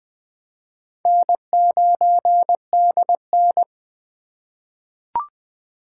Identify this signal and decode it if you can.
Morse code, heard as a keyed tone, about 20 wpm, 700 Hz. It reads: N9DN
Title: USA Call Signs - 2x Character Spacing - 20wpm